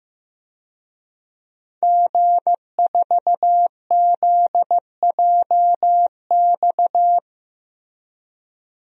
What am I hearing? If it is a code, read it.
Morse code, heard as a keyed tone, about 15 wpm, 700 Hz. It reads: G4ZJX
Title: DXCC-100 Call Signs - 5 Characters - Repeated 3x - 15wpm